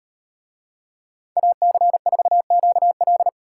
Morse code, heard as a keyed tone, about 38 wpm, 700 Hz. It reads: AC4QL